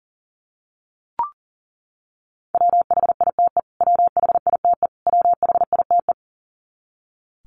Morse code, heard as a keyed tone, about 40 wpm, 700 Hz. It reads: WHITE WHITE WHITE
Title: Top 300 Words Repeated 40wpm